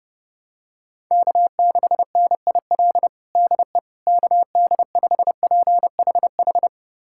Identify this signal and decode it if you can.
Morse code, heard as a keyed tone, about 32 wpm, 700 Hz. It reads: K6NIL DE KD5PHH
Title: USA Callsign DE Callsign 30wpm